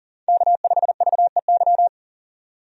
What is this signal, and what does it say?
Morse code, heard as a keyed tone, about 40 wpm, 700 Hz. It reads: K5VEY
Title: USA Call Signs - 5 Characters - 40wpm